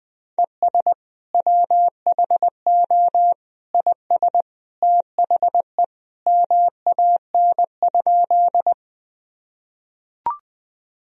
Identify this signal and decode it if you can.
Morse code, heard as a keyed tone, about 20 wpm, 700 Hz. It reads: ES WHO IS THE MAN?
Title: Sentences from Top 200 Words and Abbreviations 20wpm